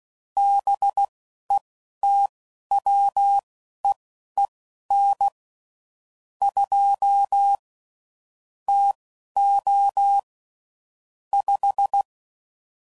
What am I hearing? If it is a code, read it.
Morse code, heard as a keyed tone, about 16 wpm, spaced as for 11 wpm, 800 Hz. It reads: BETWEEN 2 TO 5
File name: CW Code Groups @10WPM/Weather Sets